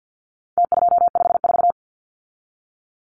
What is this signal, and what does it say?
Morse code, heard as a keyed tone, about 50 wpm, 700 Hz. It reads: T254
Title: Numbers - Contest Style - 4-Digit 50wpm